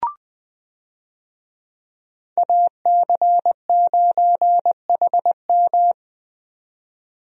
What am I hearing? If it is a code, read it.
Morse code, heard as a keyed tone, about 20 wpm, 700 Hz. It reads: AC9HM
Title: USA Call Signs - 5 Characters - 20wpm